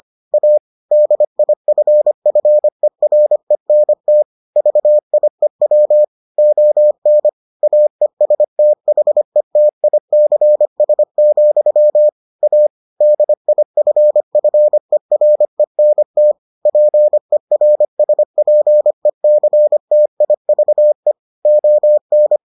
Morse code, heard as a keyed tone, about 25 wpm, 600 Hz. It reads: A DIFFERENT VIEW ON AESTHETICS, A DIFFERENT PERSPECTIVE ON